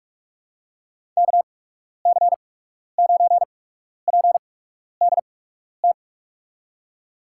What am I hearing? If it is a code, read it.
Morse code, heard as a keyed tone, about 44 wpm, spaced as for 12 wpm, 700 Hz. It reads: KC9PDT
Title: USA Call Signs - 6 Characters - 8x Character Spacing - 45wpm